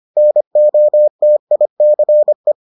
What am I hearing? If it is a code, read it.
Morse code, heard as a keyed tone, about 25 wpm, 600 Hz. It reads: NOTICE